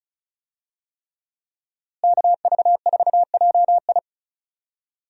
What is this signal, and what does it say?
Morse code, heard as a keyed tone, about 36 wpm, 700 Hz. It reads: KV4JI
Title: USA Call Signs - 5 Characters - Repeated 3x - 35wpm